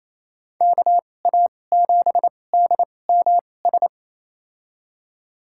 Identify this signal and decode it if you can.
Morse code, heard as a keyed tone, about 28 wpm, spaced as for 20 wpm, 700 Hz. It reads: KA7DMS
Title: USA Call Signs - 6 Characters - 2x Character Spacing - 28wpm